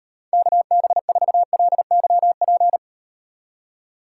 Morse code, heard as a keyed tone, about 38 wpm, 700 Hz. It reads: KB4LYP